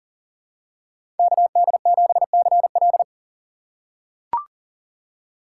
Morse code, heard as a keyed tone, about 37 wpm, 700 Hz. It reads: KD7CL
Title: USA Call Signs - 5 Characters - 3x Character Spacing - 40wpm